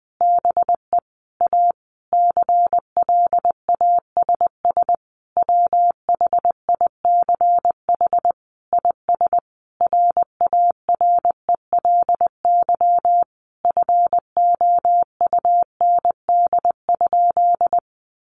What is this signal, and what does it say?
Morse code, heard as a keyed tone, about 20 wpm, 700 Hz. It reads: BE A CLASS WHICH IS RARELY FOUND?
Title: Sentences from Top 1400 Words 20wpm